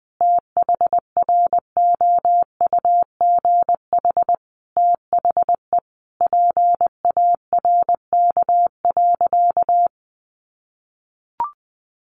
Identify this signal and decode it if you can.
Morse code, heard as a keyed tone, about 20 wpm, 700 Hz. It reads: THROUGH THE PARK.
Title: Sentences from Top 1300 Words 20wpm